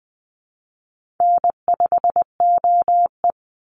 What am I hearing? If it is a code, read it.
Morse code, heard as a keyed tone, about 20 wpm, 700 Hz. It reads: N5OE
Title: Call Signs - 20wpm